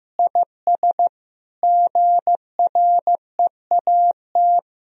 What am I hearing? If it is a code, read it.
Morse code, heard as a keyed tone, about 15 wpm, 700 Hz. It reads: IS GREAT
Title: Sentences from Top 1000 Words - Review Part 4 - 15wpm